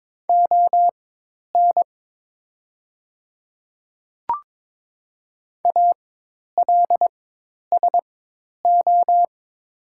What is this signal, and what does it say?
Morse code, heard as a keyed tone, about 22 wpm, spaced as for 10 wpm, 700 Hz. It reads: ON ALSO
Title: 4x Character Spacing - Top 200 Words - 22wpm